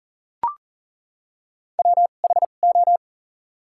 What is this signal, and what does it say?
Morse code, heard as a keyed tone, about 38 wpm, spaced as for 28 wpm, 700 Hz. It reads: WHO
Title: 2x Character Spacing - Top 100 Words - 40wpm